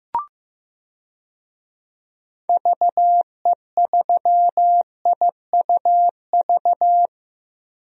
Morse code, heard as a keyed tone, about 15 wpm, 700 Hz. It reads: VE3IUV
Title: DXCC-100 Call Signs - 6 Characters - 15wpm